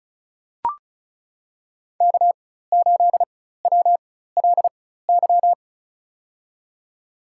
Morse code, heard as a keyed tone, about 36 wpm, spaced as for 16 wpm, 700 Hz. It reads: K8WLY
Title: USA Call Signs - 5 Characters - 4x Character Spacing - 35wpm